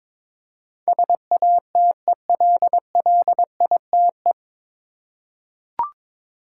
Morse code, heard as a keyed tone, about 22 wpm, 700 Hz. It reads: SATELLITE
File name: 9 Letter Words - First 3 Letters Given - 22wpm